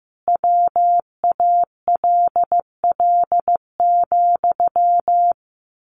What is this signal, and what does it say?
Morse code, heard as a keyed tone, about 15 wpm, 700 Hz. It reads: WALL,
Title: A Sound Of Thunder 15wpm